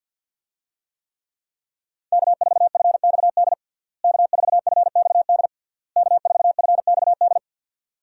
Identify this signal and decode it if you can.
Morse code, heard as a keyed tone, about 50 wpm, 700 Hz. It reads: K4FXD K4FXD K4FXD
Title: USA Call Signs - 5 Characters - Repeated 3x - 50wpm